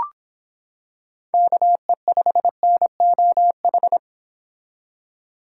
Morse code, heard as a keyed tone, about 26 wpm, 700 Hz. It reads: KE5NOH